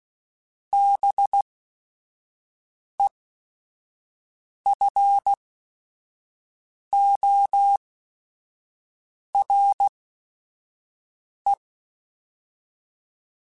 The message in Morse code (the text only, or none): BEFORE